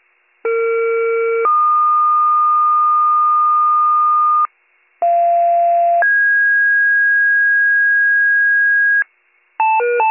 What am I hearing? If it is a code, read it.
Two-tone page, heard as a one-second tone, then a three-second tone, about 460 then 1220 Hz, about 690 then 1650 Hz, with noise. It is heard from 0.4 to 9.0 s.